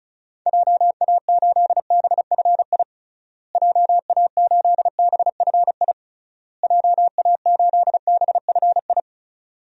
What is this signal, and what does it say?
Morse code, heard as a keyed tone, about 34 wpm, 700 Hz. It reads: JA8BFI JA8BFI JA8BFI
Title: DXCC-100 Call Signs - 6 Characters - Repeated 3x - 35wpm